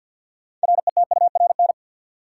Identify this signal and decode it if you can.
Morse code, heard as a keyed tone, about 46 wpm, 700 Hz. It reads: RETURN
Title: Sentences from Top 800 Words - New Word Drill - 50wpm